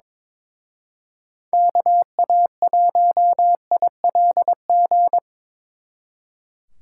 Morse code, heard as a keyed tone, about 24 wpm, 700 Hz. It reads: KA1ILG
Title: Speed Racing - USA 2x3 Call Signs 22wpm